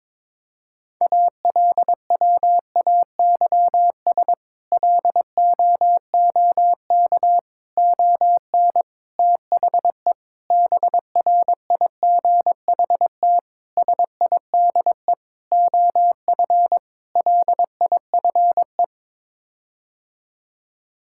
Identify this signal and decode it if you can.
Morse code, heard as a keyed tone, about 22 wpm, 700 Hz. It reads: ALWAYS LOOK ON THE BRIGHT SIDE OF LIFE
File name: practice / 22wpm / Sets of 8 Words - 22wpm